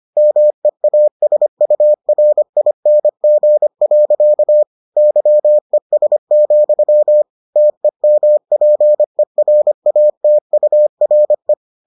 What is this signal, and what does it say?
Morse code, heard as a keyed tone, about 25 wpm, 600 Hz. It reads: MEASURING. YES, TEMPERATURE